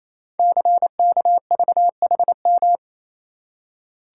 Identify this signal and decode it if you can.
Morse code, heard as a keyed tone, about 28 wpm, 700 Hz. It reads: CKVHM
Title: Code Groups - 5 Letters - 28wpm